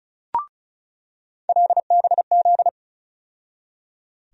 Morse code, heard as a keyed tone, about 36 wpm, 700 Hz. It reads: LBZ
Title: USA Call Sign Postfixes - 3 Letters 35wpm